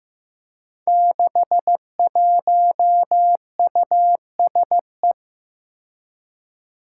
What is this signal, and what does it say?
Morse code, heard as a keyed tone, about 15 wpm, 700 Hz. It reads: 61USE
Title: Five Challenging Letter-Number Combinations 15wpm